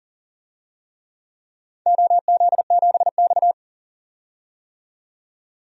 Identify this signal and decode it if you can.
Morse code, heard as a keyed tone, about 42 wpm, 700 Hz. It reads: OZ7X
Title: DXCC-100 Call Signs - 4 Characters - 40wpm